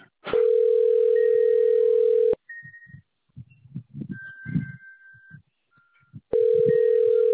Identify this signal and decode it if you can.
Ringback tone (North America), 440 + 480 Hz.